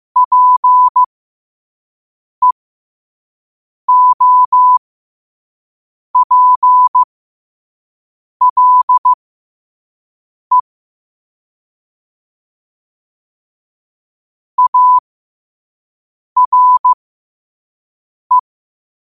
Morse code, heard as a keyed tone, about 15 wpm, spaced as for 5 wpm, 1000 Hz. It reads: PEOPLE ARE
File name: qotdfiles / 5wpm